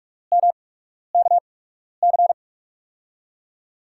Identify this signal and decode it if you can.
Morse code, heard as a keyed tone, about 43 wpm, spaced as for 12 wpm, 700 Hz. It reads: MKC
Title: USA Call Signs - 6 Characters - 8x Character Spacing - 45wpm